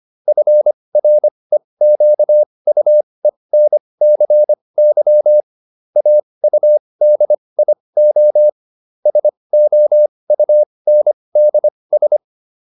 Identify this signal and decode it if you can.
Morse code, heard as a keyed tone, about 25 wpm, spaced as for 20 wpm, 600 Hz. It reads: FREQUENCY AUDIO SOUNDS